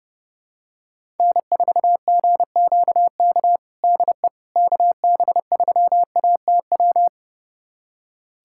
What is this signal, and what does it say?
Morse code, heard as a keyed tone, about 32 wpm, 700 Hz. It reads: N4GQK DE KB3ATW
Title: USA Callsign DE Callsign 30wpm